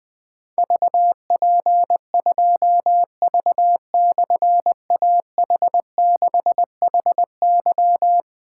Morse code, heard as a keyed tone, about 20 wpm, 700 Hz. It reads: VP2V/AH6HY
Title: Difficult Call Signs 20wpm